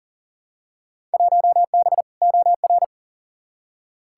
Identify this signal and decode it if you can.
Morse code, heard as a keyed tone, about 37 wpm, 700 Hz. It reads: 1B OR
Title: Field Day Exchanges - 40wpm